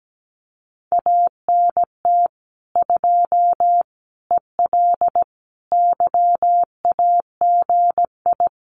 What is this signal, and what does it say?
Morse code, heard as a keyed tone, about 17 wpm, 700 Hz. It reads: ANT 2 EL YAGI
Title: Common QSO Phrases Encore 17wpm